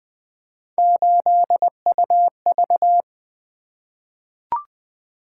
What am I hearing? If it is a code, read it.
Morse code, heard as a keyed tone, about 20 wpm, 700 Hz. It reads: 8UV
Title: Three Letter-Number Combinations - 20wpm